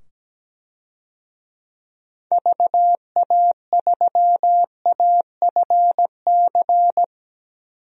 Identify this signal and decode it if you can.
Morse code, heard as a keyed tone, about 17 wpm, 700 Hz. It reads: VA3AFC
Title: DXCC-100 Call Signs - 6 Characters - 17wpm